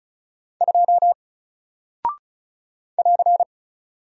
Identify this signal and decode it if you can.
Morse code, heard as a keyed tone, about 36 wpm, 700 Hz. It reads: 2 <AR>
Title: Single Character (Letter:Number:Punctuation:Prosign) Alt 35wpm